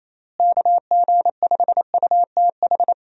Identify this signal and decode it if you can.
Morse code, heard as a keyed tone, about 28 wpm, 700 Hz. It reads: KG5UTH
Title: USA Call Signs - 6 Characters - Repeated 3x - 28wpm